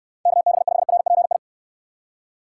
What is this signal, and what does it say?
Morse code, heard as a keyed tone, about 64 wpm, 700 Hz. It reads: KB5RPI